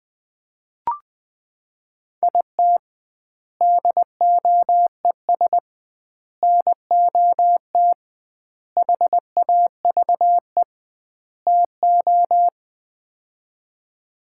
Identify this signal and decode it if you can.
Morse code, heard as a keyed tone, about 21 wpm, 700 Hz. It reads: IT DOES NOT HAVE TO
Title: 2x Word-Spacing - Sets of 5 Words - 20wpm